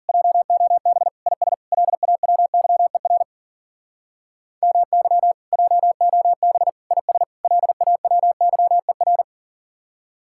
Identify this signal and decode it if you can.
Morse code, heard as a keyed tone, about 50 wpm, 700 Hz. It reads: JOB IS LAWYER MY JOB IS LAWYER